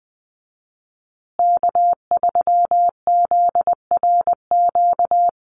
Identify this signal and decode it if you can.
Morse code, heard as a keyed tone, about 20 wpm, 700 Hz. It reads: K3ZRQ
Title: USA Call Signs Repeated 3x 20wpm